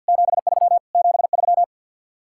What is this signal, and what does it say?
Morse code, heard as a keyed tone, about 65 wpm, spaced as for 57 wpm, 700 Hz. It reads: 73 73
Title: Speed-Racing Common QSO Phrases Encore - 50wpm